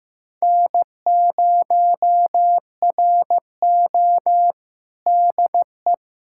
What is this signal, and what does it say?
Morse code, heard as a keyed tone, about 15 wpm, 700 Hz. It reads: N0RO DE
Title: USA Callsign DE Callsign 15wpm